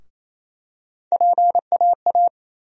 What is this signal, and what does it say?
Morse code, heard as a keyed tone, about 28 wpm, 700 Hz. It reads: PAA
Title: USA Call Sign Postfixes - 3 Letters 28wpm